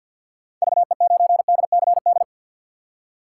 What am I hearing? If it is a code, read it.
Morse code, heard as a keyed tone, about 47 wpm, 690 Hz. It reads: VE9DCD
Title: K1USN SST QSO Practice - 50wpm